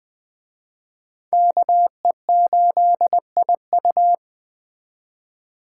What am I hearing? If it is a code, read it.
Morse code, heard as a keyed tone, about 24 wpm, 700 Hz. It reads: KE8IU
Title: Speed Racing - USA 2x2 Call Signs 20wpm